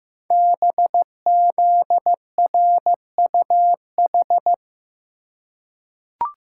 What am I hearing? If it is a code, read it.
Morse code, heard as a keyed tone, about 15 wpm, 700 Hz. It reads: BZRUH
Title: Code Groups - 5 Letters - 15wpm